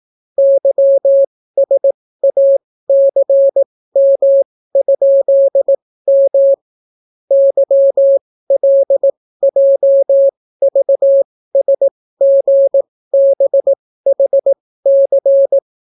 Morse code, heard as a keyed tone, about 18 wpm, spaced as for 15 wpm, 550 Hz. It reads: YSACM?M YLJVSGBHC